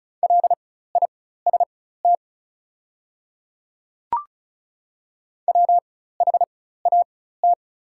Morse code, heard as a keyed tone, about 36 wpm, spaced as for 16 wpm, 690 Hz. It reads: LIST WHAT
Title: 4x Character Spacing - Top 500 Words - 35wpm